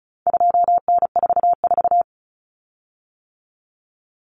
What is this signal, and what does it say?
Morse code, heard as a keyed tone, about 34 wpm, 700 Hz. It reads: 2N44
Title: Common Cut Numbers - 4-Digit 35wpm